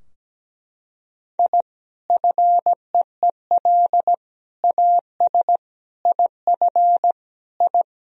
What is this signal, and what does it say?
Morse code, heard as a keyed tone, about 17 wpm, 700 Hz. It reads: I FEEL AS IF I
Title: Sentences from Top 1000 Words - Review Part 3 - 17wpm